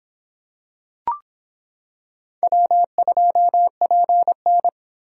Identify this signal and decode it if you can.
Morse code, heard as a keyed tone, about 25 wpm, 700 Hz. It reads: W2PN